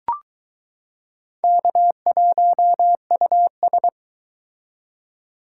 Morse code, heard as a keyed tone, about 22 wpm, 700 Hz. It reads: K1US